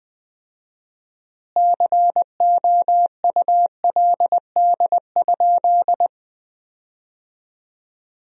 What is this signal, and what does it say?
Morse code, heard as a keyed tone, about 21 wpm, 700 Hz. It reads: COULD?
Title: 4x Word-Spacing - Sentences from Top 400 Words - 20wpm